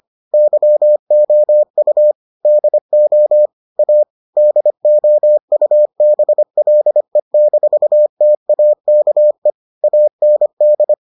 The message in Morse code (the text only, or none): YOU DO A DOUBLE-TAKE AND